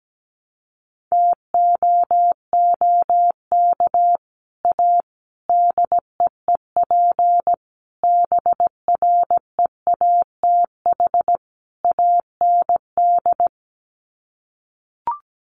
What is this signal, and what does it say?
Morse code, heard as a keyed tone, about 17 wpm, 690 Hz. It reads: TOOK A DEEP BREATH AND
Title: Sets of 5 Words Encore 17wpm